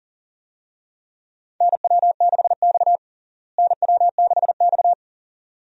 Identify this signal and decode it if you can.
Morse code, heard as a keyed tone, about 40 wpm, 700 Hz. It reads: NW6X NW6X
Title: USA Call Signs - 4 Characters - Repeated 2x - 40wpm